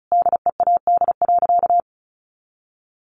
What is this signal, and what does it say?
Morse code, heard as a keyed tone, about 34 wpm, 700 Hz. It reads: DEAD.
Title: Sentences from Top 1500 Words 35wpm